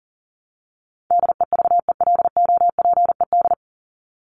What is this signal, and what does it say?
Morse code, heard as a keyed tone, about 40 wpm, 700 Hz. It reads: DEVELOPED
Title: ED Suffix - Top 100 Words with Base Words 40wpm